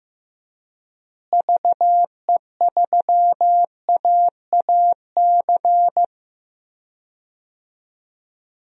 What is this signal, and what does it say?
Morse code, heard as a keyed tone, about 15 wpm, 700 Hz. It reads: VE3AAC
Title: DXCC-100 Call Signs - 6 Characters - Repeated 3x - 15wpm